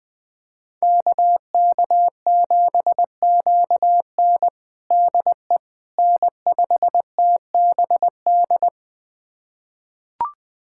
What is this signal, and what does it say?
Morse code, heard as a keyed tone, about 20 wpm, 700 Hz. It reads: KK7QN DE N5TBD
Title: USA Callsign DE Callsign - 5 Characters - 20wpm